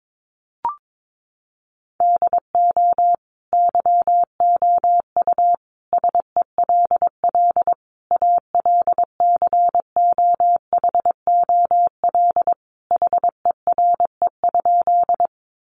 Morse code, heard as a keyed tone, about 22 wpm, 700 Hz. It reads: DO YOU SELL ALCOHOL HERE?
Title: Sentences from Top 1400 Words 22wpm